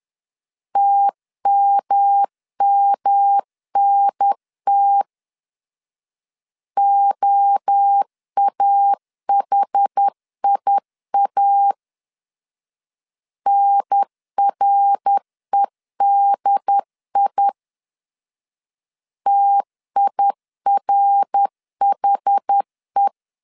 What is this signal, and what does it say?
Morse code, heard as a keyed tone, about 10 wpm, 800 Hz. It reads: TMMNT OAHIA NREDI TIRHE
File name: CW / DET / 1 3 1500C S B 10wpm 3335